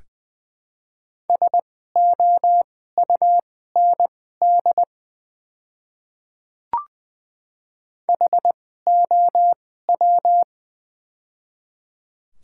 Morse code, heard as a keyed tone, about 20 wpm, spaced as for 14 wpm, 700 Hz. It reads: SOUND HOW
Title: 2x Character Spacing - Top 200 Words - 20wpm